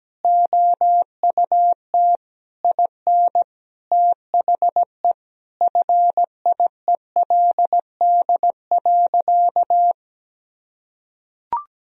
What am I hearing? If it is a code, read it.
Morse code, heard as a keyed tone, about 17 wpm, 700 Hz. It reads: OUT IN THE FIELD.
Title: Sentences from Top 1000 Words - Review Part 4 - 17wpm